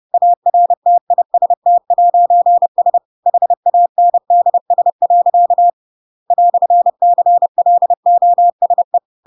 Morse code, heard as a keyed tone, about 30 wpm, 700 Hz. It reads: ARTIST'S HANDS. "CLOSE
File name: Pinocchio / Pinocchio ch8 30 30 ik1hge